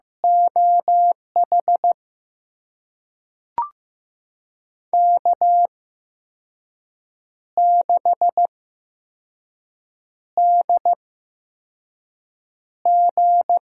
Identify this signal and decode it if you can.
Morse code, heard as a keyed tone, about 15 wpm, spaced as for 5 wpm, 700 Hz. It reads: OH K6DG
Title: USA Call Signs - 8x Character Spacing - 15wpm